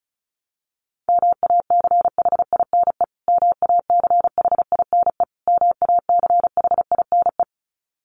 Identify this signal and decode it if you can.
Morse code, heard as a keyed tone, about 34 wpm, 700 Hz. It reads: MACHINE MACHINE MACHINE
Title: Top 500 Words Repeated 35wpm